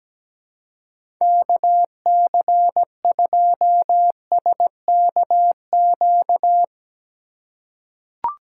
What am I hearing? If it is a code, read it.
Morse code, heard as a keyed tone, about 17 wpm, 700 Hz. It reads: KC2SKQ
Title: USA Call Signs - 4x Character Spacing - 17wpm